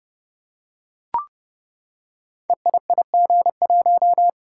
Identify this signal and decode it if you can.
Morse code, heard as a keyed tone, about 30 wpm, 700 Hz. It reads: EIIG1
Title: Code Groups - 5 Letter:Numbers - 30wpm